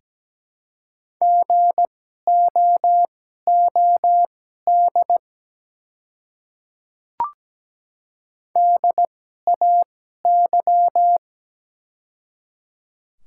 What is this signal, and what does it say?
Morse code, heard as a keyed tone, about 17 wpm, spaced as for 12 wpm, 690 Hz. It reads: GOOD DAY
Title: 2x Character Spacing - Top 300 Words - 17wpm